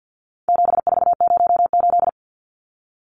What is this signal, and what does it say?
Morse code, heard as a keyed tone, about 50 wpm, 700 Hz. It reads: 7408
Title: Numbers - 4-Digit 50wpm